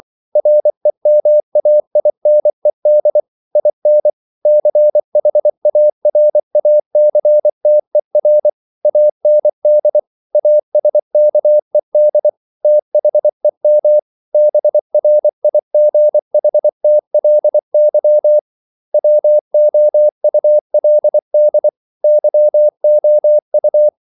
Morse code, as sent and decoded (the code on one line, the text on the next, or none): .-. -- .- .. -. -.. / .. -. / -.-. .... .- .-. .- -.-. - . .-. / .- -. -.. / .- ... -.- . -.. / - .... -- / -... .-. .. --. .... - .-.. -.-- / .-- --- ..- .-.. -.. / -.-- --- ..-
REMAINED IN CHARACTER AND ASKED THEM BRIGHTLY WOULD YOU